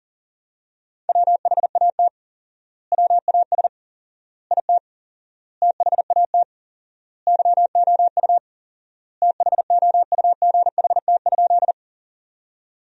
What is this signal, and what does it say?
Morse code, heard as a keyed tone, about 40 wpm, 700 Hz. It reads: WHAT WAS IT THAT YOU THOUGHT?
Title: 4x Word-Spacing - Sentences from Top 300 Words - 40wpm